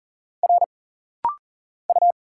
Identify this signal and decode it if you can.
Morse code, heard as a keyed tone, about 40 wpm, 700 Hz. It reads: R U
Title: Single Letter-Number Alt - Rapid-Fire 40wpm